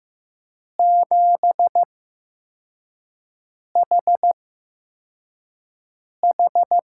Morse code, heard as a keyed tone, about 15 wpm, spaced as for 4 wpm, 700 Hz. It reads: 7HH
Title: USA Call Signs - 6 Characters - 8x Character Spacing - 15wpm